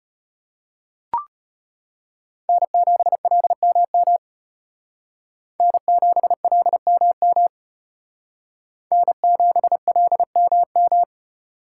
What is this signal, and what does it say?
Morse code, heard as a keyed tone, about 32 wpm, 700 Hz. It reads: N7LMM N7LMM N7LMM